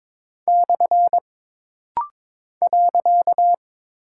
Morse code, heard as a keyed tone, about 23 wpm, 700 Hz. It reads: / .
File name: practice / 22wpm / Single Punctuation-Prosign - Rapid-Fire - 22wpm